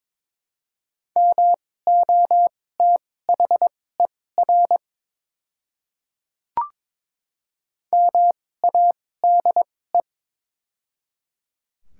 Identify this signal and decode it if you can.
Morse code, heard as a keyed tone, about 22 wpm, spaced as for 16 wpm, 700 Hz. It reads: MOTHER MADE
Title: 2x Character Spacing - Top 200 Words - 22wpm